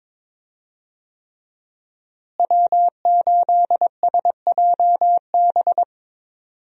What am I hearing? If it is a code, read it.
Morse code, heard as a keyed tone, about 22 wpm, 700 Hz. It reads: W8SJB
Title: USA Call Signs - 5 Characters - Repeated 3x - 22wpm